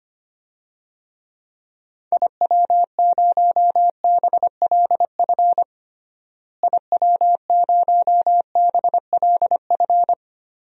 Morse code, heard as a keyed tone, about 25 wpm, 700 Hz. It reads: IW0BLF IW0BLF
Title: DXCC-100 Call Signs - 6 Characters - Repeated 3x - 25wpm